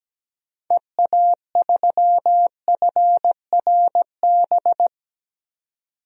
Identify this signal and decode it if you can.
Morse code, heard as a keyed tone, about 17 wpm, 700 Hz. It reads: EA3FRB
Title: DXCC-100 Call Signs - 6 Characters - 17wpm